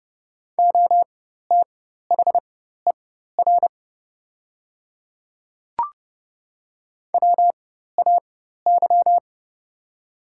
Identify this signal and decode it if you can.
Morse code, heard as a keyed tone, about 30 wpm, spaced as for 14 wpm, 700 Hz. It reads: OTHER WAY
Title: 4x Character Spacing - Top 100 Words - 30wpm